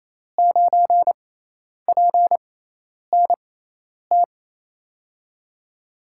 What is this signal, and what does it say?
Morse code, heard as a keyed tone, about 28 wpm, spaced as for 10 wpm, 700 Hz. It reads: 9PNT
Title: USA Call Signs - 6 Characters - 6x Character Spacing - 28wpm